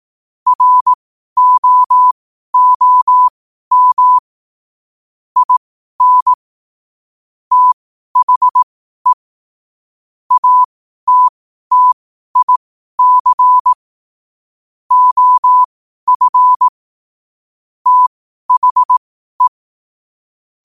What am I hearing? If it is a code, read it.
Morse code, heard as a keyed tone, about 18 wpm, spaced as for 12 wpm, 1000 Hz. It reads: ROOM IN THE ATTIC OF THE
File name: qotdfiles / 12wpm